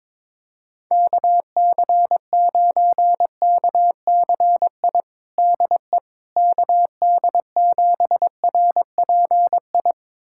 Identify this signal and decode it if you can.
Morse code, heard as a keyed tone, about 22 wpm, 700 Hz. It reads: KC9KCI DE KD7RPI
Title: USA Callsign DE Callsign 22wpm